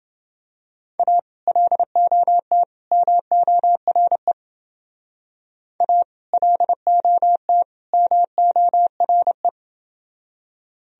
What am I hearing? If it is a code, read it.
Morse code, heard as a keyed tone, about 28 wpm, 700 Hz. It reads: A LOT MORE A LOT MORE